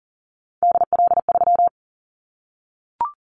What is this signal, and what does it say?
Morse code, heard as a keyed tone, about 40 wpm, 700 Hz. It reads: DL3
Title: Callsign Prefixes for Top 25 DXCC - 40wpm